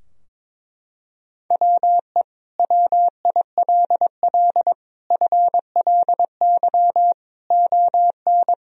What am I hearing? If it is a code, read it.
Morse code, heard as a keyed tone, about 22 wpm, 700 Hz. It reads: WE WILL FLY ON
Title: Sentences from Top 1000 Words - Review Part 3 - 22wpm